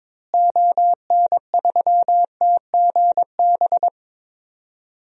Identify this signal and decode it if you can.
Morse code, heard as a keyed tone, about 21 wpm, 700 Hz. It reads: ON3TGB
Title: DXCC-100 Call Signs - 6 Characters - Repeated 2x - 22wpm